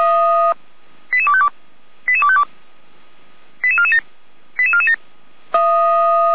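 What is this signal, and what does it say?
Five-tone selective call (ZVEI), heard as quick runs of five tones, tones of about 70 ms. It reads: 88252 88252 88447 88447
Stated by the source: ZVEI five-tone selective call 88255, received on SDR.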